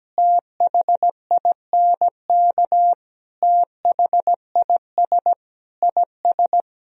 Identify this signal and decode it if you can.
Morse code, heard as a keyed tone, about 17 wpm, 700 Hz. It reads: THINK THIS IS
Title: Sentences from Top 1000 Words - Review Part 4 - 17wpm